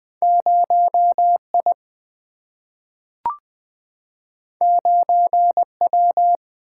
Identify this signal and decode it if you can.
Morse code, heard as a keyed tone, about 20 wpm, 700 Hz. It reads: 0I 9W
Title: Single-Letter Single-Number 20wpm